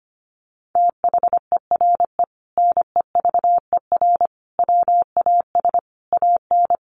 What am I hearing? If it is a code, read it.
Morse code, heard as a keyed tone, about 25 wpm, 700 Hz. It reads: THERE NEVER WAS AN
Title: Sentences from Top 1500 Words 25wpm